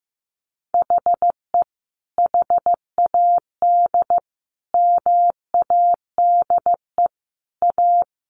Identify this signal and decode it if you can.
Morse code, heard as a keyed tone, about 15 wpm, 700 Hz. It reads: HE HAD MADE A
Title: Sentences from Top 900 Words 15wpm